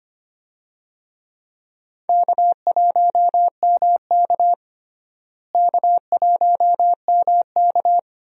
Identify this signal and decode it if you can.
Morse code, heard as a keyed tone, about 25 wpm, 700 Hz. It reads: K1MK K1MK
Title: USA Call Signs - 4 Characters - Repeated 3x - 25wpm